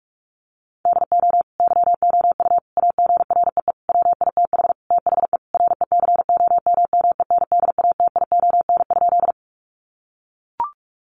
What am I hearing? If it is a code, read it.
Morse code, heard as a keyed tone, about 44 wpm, 700 Hz. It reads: DO YOU AGREE WITH THE RECOMMENDATION?
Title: Sentences from Top 1400 Words 45wpm